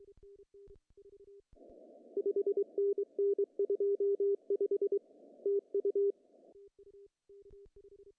Morse code, heard as a keyed tone, about 22 wpm, 380 Hz. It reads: 5NN 25 TU
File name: CW / cqww-cw-k5tr